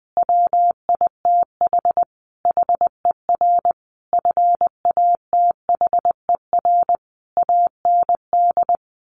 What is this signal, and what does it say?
Morse code, heard as a keyed tone, about 20 wpm, 700 Hz. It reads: WITH HER FATHER AND
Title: Sentences from Top 1100 Words 20wpm